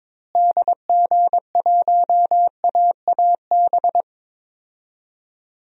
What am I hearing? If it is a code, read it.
Morse code, heard as a keyed tone, about 22 wpm, 700 Hz. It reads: DG1AAB
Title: DXCC-100 Call Signs - 6 Characters - Repeated 2x - 22wpm